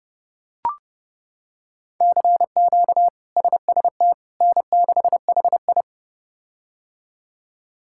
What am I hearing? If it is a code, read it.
Morse code, heard as a keyed tone, about 30 wpm, 700 Hz. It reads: CQ SST N6HI
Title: K1USN SST QSO Practice - 30wpm